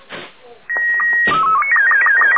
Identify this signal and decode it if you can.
SSTV (slow-scan television), Robot 36.